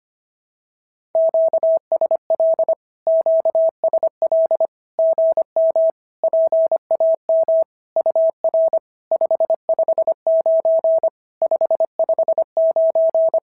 Morse code, heard as a keyed tone, about 25 wpm, 650 Hz. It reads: QSL QSL GM PAM UR 559 559
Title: SOTA QSO Practice 25wpm